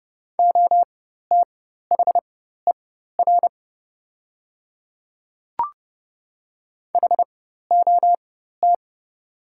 Morse code, heard as a keyed tone, about 30 wpm, spaced as for 14 wpm, 700 Hz. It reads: OTHER HOT
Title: 4x Character Spacing - Top 100 Words - 30wpm